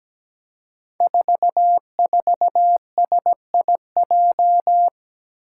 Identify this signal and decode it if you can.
Morse code, heard as a keyed tone, about 17 wpm, 700 Hz. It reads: 44SIJ
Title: Five Challenging Letter-Number Combinations 17wpm